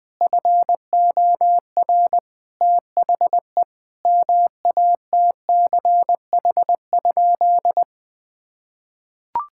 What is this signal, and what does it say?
Morse code, heard as a keyed tone, about 21 wpm, 700 Hz. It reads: FOR THE MATCH?
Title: Sentences from Top 1600 Words 20wpm